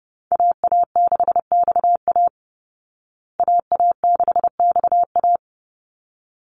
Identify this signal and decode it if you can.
Morse code, heard as a keyed tone, about 30 wpm, 700 Hz. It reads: AA6XA AA6XA
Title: Call Signs Repeated 3x - 30wpm